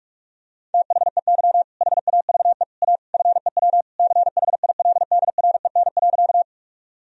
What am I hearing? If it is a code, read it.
Morse code, heard as a keyed tone, about 46 wpm, 700 Hz. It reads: THEY HAVE A FEW CHILDREN.
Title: Sentences from Top 1000 Words - Review Part 1 - 45wpm